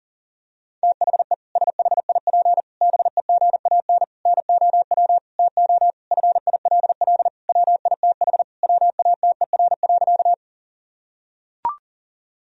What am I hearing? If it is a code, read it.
Morse code, heard as a keyed tone, about 40 wpm, 700 Hz. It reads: THE SHIP BEGAN NOW TO FILL WITH WATER.
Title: Sentences from Top 1000 Words - Review Part 3 - 40wpm